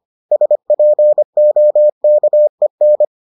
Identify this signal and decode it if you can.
Morse code, heard as a keyed tone, about 25 wpm, 600 Hz. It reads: SPOKEN